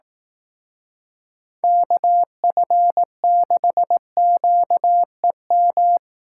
Morse code, heard as a keyed tone, about 18 wpm, 700 Hz. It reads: KF6QEM
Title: Speed Racing - USA 2x3 Call Signs 15wpm